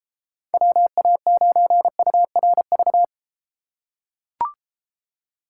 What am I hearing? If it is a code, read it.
Morse code, heard as a keyed tone, about 33 wpm, 700 Hz. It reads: WA9URV